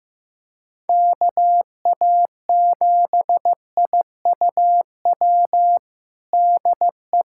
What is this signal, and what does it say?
Morse code, heard as a keyed tone, about 15 wpm, 700 Hz. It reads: KA7IUW DE
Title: USA Callsign DE Callsign - 6 Characters - 15wpm